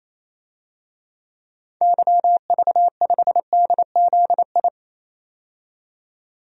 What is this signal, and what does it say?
Morse code, heard as a keyed tone, about 28 wpm, 700 Hz. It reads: YV5DZI
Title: DXCC-100 Call Signs - 6 Characters - 28wpm